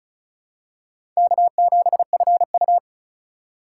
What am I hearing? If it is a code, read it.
Morse code, heard as a keyed tone, about 34 wpm, 700 Hz. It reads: K7FU
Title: USA Call Signs - 4 Characters - 2x Character Spacing - 35wpm